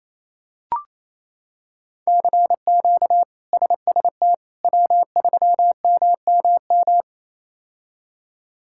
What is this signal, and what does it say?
Morse code, heard as a keyed tone, about 28 wpm, 700 Hz. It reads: CQ SST W3MMM
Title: K1USN SST QSO Practice - 28wpm